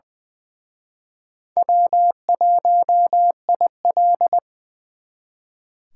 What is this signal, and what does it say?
Morse code, heard as a keyed tone, about 24 wpm, 700 Hz. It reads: W1IL
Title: Speed Racing - USA 1x2 Call Signs 20wpm